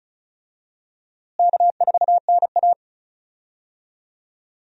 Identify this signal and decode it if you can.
Morse code, heard as a keyed tone, about 34 wpm, 700 Hz. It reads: K4NA
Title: USA Call Signs - 4 Characters - Repeated 2x - 35wpm